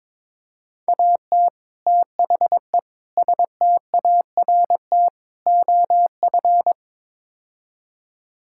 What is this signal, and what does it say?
Morse code, heard as a keyed tone, about 22 wpm, 700 Hz. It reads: AT THE START OF
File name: practice / 22wpm / Build A Phrase - 5-Word Phrases - 22wpm